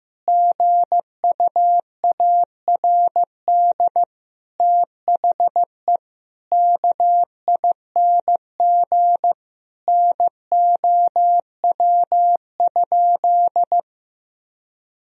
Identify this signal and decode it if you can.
Morse code, heard as a keyed tone, about 15 wpm, 700 Hz. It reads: GUARD THE KING NOW?
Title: Sentences from Top 1000 Words - Review Part 4 - 15wpm